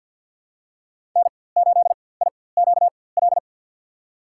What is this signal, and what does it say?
Morse code, heard as a keyed tone, about 52 wpm, spaced as for 23 wpm, 700 Hz. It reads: N8IQL
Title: USA Call Signs - 5 Characters - 4x Character Spacing - 50wpm